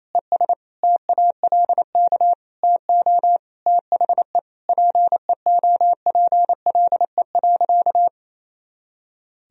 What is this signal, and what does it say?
Morse code, heard as a keyed tone, about 28 wpm, 700 Hz. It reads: ES TALK TO THE PEOPLE.
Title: Sentences from Top 400 Words and Abbreviations 28wpm